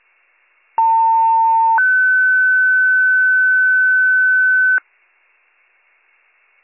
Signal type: two-tone page

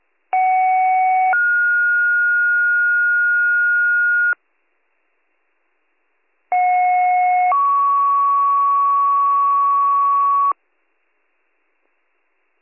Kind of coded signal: two-tone page

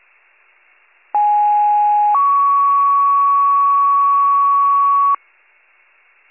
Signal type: two-tone page

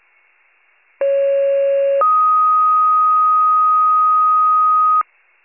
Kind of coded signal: two-tone page

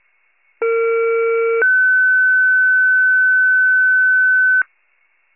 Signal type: two-tone page

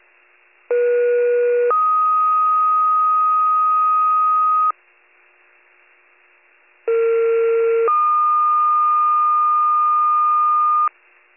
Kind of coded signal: two-tone page

